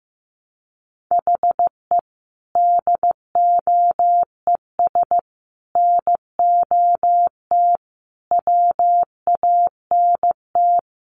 Morse code, heard as a keyed tone, about 14 wpm, 700 Hz. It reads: HE DOES NOT WANT
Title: Sets of 5 Words 15wpm